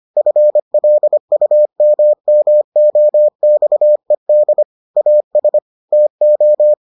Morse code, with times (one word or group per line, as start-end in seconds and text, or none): 0.2-4.6 FLUMMOXED
5.0-5.6 AS
5.9-6.7 TO